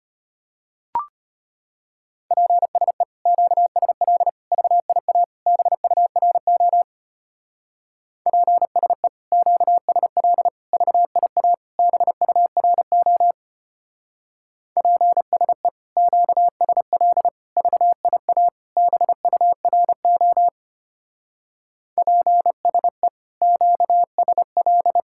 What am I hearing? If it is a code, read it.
Morse code, heard as a keyed tone, about 30 wpm, 700 Hz. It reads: PSE QSL VIA BURO PSE QSL VIA BURO PSE QSL VIA BURO PSE QSL
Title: Speed-Racing Common QSO Phrases Encore - 25wpm